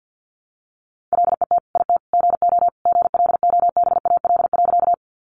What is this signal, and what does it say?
Morse code, heard as a keyed tone, about 50 wpm, 700 Hz. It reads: LET IT GO GLOBAL.
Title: Sentences from Top 1200 Words 50wpm